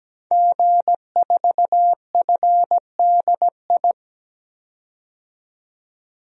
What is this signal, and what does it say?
Morse code, heard as a keyed tone, about 17 wpm, 700 Hz. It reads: G4FDI
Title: DXCC-100 Call Signs - 5 Characters - 17wpm